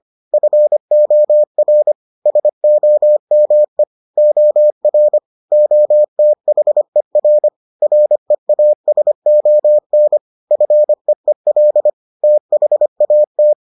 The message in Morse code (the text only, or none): FOR SOME OR OTHER REASON FEEL THAT